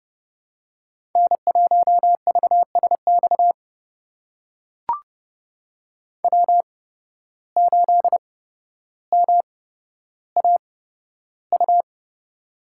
Morse code, heard as a keyed tone, about 30 wpm, spaced as for 13 wpm, 700 Hz. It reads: N1VSX W8MAU
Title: USA Call Signs - 5 Characters - 8x Character Spacing - 30wpm